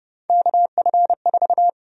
Morse code, heard as a keyed tone, about 30 wpm, 700 Hz. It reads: KF4
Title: USA Call Sign Prefixes - 30wpm